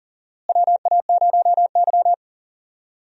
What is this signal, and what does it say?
Morse code, heard as a keyed tone, about 40 wpm, 700 Hz. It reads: WA0Y
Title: USA Call Signs - 4 Characters - Repeated 3x - 40wpm